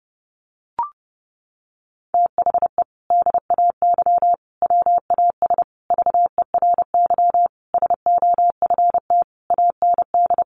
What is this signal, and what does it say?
Morse code, heard as a keyed tone, about 30 wpm, 700 Hz. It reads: THE DAY WAS VERY SOFT AND
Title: Sentences from Top 1500 Words 30wpm